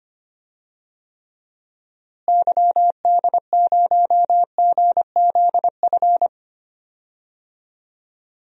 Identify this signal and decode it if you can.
Morse code, heard as a keyed tone, about 25 wpm, 700 Hz. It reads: YD0GZF
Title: DXCC-100 Call Signs - 6 Characters - Repeated 3x - 25wpm